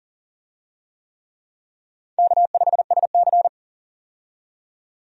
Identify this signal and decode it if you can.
Morse code, heard as a keyed tone, about 38 wpm, 700 Hz. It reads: K5SC
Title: USA Call Signs - 4 Characters - 40wpm